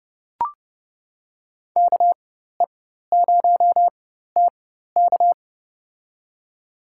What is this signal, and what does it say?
Morse code, heard as a keyed tone, about 30 wpm, spaced as for 14 wpm, 700 Hz. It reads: KE0TK
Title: USA Call Signs - 5 Characters - 4x Character Spacing - 30wpm